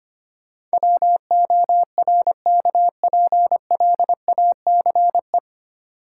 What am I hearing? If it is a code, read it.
Morse code, heard as a keyed tone, about 25 wpm, 700 Hz. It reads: WORKPLACE
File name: practice / 25wpm / 9 Letter Words - First 3 Letters Given - 25wpm